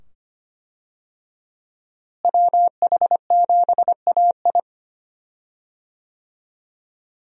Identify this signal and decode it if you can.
Morse code, heard as a keyed tone, about 25 wpm, 700 Hz. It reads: WH7AI
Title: USA Call Signs - 5 Characters - Repeated 2x - 25wpm